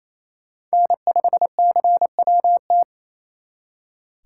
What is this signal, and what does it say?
Morse code, heard as a keyed tone, about 32 wpm, 700 Hz. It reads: N5CWT
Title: Speed Racing - USA 1x3 Call Signs 28wpm